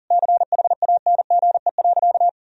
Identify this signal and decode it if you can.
Morse code, heard as a keyed tone, about 40 wpm, 700 Hz. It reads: CHANGE.
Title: Sentences from Top 1000 Words - Review Part 1 - 40wpm